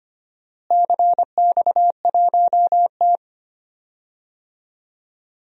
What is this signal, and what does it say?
Morse code, heard as a keyed tone, about 25 wpm, 700 Hz. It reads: CX1T
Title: DXCC-100 Call Signs - 4 Characters - Repeated 2x - 25wpm